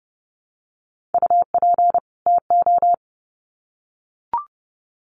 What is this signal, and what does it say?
Morse code, heard as a keyed tone, about 30 wpm, 700 Hz. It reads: UP TO
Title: Sets of 2 Words 30wpm